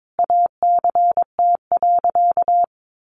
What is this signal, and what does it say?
Morse code, heard as a keyed tone, about 22 wpm, 700 Hz. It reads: ACT.
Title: Sentences from Top 1400 Words 22wpm